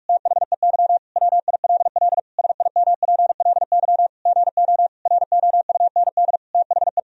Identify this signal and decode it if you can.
Morse code, heard as a keyed tone, about 46 wpm, 700 Hz. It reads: THEY WILL SIMPLY GO ROUND THE
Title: Sentences from Top 1000 Words - Review Part 4 - 45wpm